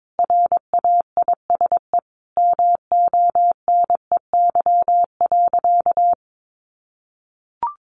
Morse code, heard as a keyed tone, about 22 wpm, 700 Hz. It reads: RAISE MONEY.
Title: Sentences from Top 1400 Words 22wpm